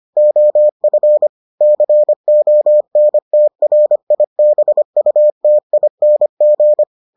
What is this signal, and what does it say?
Morse code, heard as a keyed tone, about 25 wpm, 600 Hz. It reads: OF CONTRIBUTING